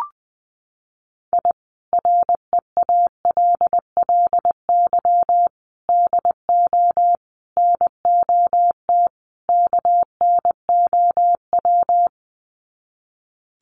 Morse code, heard as a keyed tone, about 20 wpm, 700 Hz. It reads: I REALLY DO NOT KNOW
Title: Sets of 5 Words Encore 20wpm